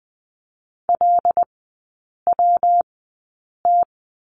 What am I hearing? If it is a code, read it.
Morse code, heard as a keyed tone, about 19 wpm, spaced as for 8 wpm, 700 Hz. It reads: LWT
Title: 184 Lesson 30e - Letter b - Three Characters 20wpm